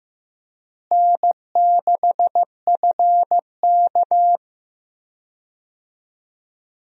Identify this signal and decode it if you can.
Morse code, heard as a keyed tone, about 15 wpm, 700 Hz. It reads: N6FK
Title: USA Call Signs - 4 Characters - Repeated 3x - 15wpm